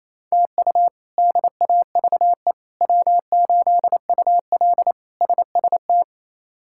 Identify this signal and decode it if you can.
Morse code, heard as a keyed tone, about 28 wpm, 700 Hz. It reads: TU DAVE W8UL SST
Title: K1USN SST QSO Practice - 28wpm